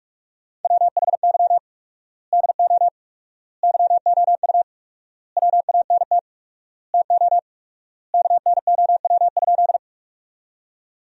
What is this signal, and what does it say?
Morse code, heard as a keyed tone, about 41 wpm, 700 Hz. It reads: WHY DO YOU WANT TO KNOW?
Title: 4x Word-Spacing - Sentences from Top 200 Words - 45wpm